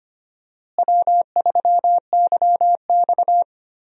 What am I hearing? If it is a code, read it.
Morse code, heard as a keyed tone, about 25 wpm, 700 Hz. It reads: W3YX
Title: USA Call Signs - 4 Characters - Repeated 3x - 25wpm